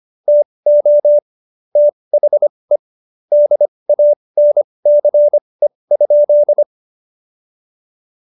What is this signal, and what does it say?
Morse code, heard as a keyed tone, about 25 wpm, spaced as for 20 wpm, 590 Hz. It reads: TO THE DANCE?
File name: Jokes-25-char-20-eff